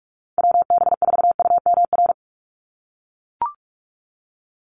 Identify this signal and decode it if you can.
Morse code, heard as a keyed tone, about 44 wpm, 700 Hz. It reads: WB4UMR
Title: USA Call Signs 45wpm